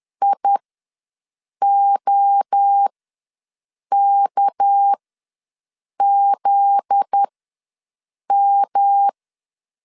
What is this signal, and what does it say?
Morse code, heard as a keyed tone, about 10 wpm, spaced as for 6 wpm, 800 Hz. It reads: IOKZM
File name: LEZIONI / 7 GF9Z / 7 GF9Z ws 10wpm 3919